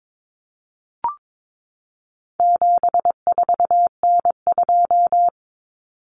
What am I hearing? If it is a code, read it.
Morse code, heard as a keyed tone, about 22 wpm, 700 Hz. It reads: 74N2
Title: Common Cut Numbers - 4-Digit 22wpm